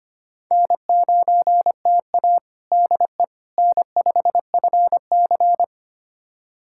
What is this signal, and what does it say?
Morse code, heard as a keyed tone, about 25 wpm, 700 Hz. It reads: N9TA DE N5FC
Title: USA Callsign DE Callsign - 4 Characters - 25wpm